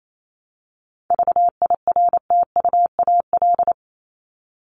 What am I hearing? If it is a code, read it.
Morse code, heard as a keyed tone, about 28 wpm, 700 Hz. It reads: VIRTUAL
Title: LY Suffix - Top 100 Words with Base Words 28wpm